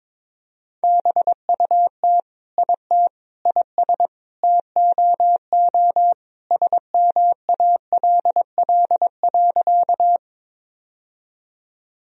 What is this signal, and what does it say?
Morse code, heard as a keyed tone, about 21 wpm, 700 Hz. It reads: BUT IT IS TOO SMALL.
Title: Sentences from Top 1000 Words - Review Part 1 - 22wpm